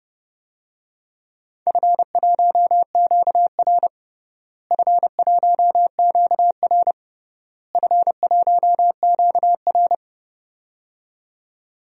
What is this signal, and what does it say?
Morse code, heard as a keyed tone, about 32 wpm, 700 Hz. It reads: F1QR F1QR F1QR
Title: DXCC-100 Call Signs - 4 Characters - Repeated 3x - 30wpm